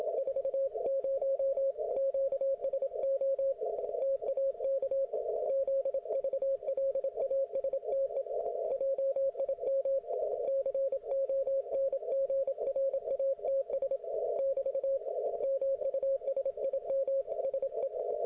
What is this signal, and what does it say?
Morse code, heard as a keyed tone, about 28 wpm, 540 Hz. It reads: 40 QSO TAK ZVLASTE JIM CONGRATS <BT> MUSIMSE